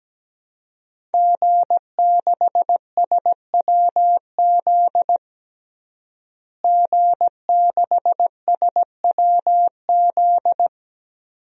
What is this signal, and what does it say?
Morse code, heard as a keyed tone, about 17 wpm, 700 Hz. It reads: G6SWZ G6SWZ
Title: DXCC-100 Call Signs - 5 Characters - Repeated 2x - 17wpm